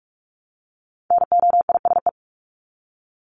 Morse code, heard as a keyed tone, about 44 wpm, 700 Hz. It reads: NOISE
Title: Top 1501-1750 Words 45wpm